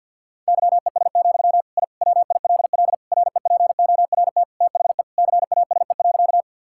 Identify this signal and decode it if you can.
Morse code, heard as a keyed tone, about 52 wpm, 700 Hz. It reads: YES, I WILL REPORT THE CASE.
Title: Sentences from Top 1000 Words - Review Part 6 - 50wpm